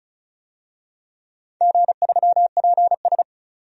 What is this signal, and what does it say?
Morse code, heard as a keyed tone, about 36 wpm, 700 Hz. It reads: G3PS
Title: DXCC-100 Call Signs - 4 Characters - 35wpm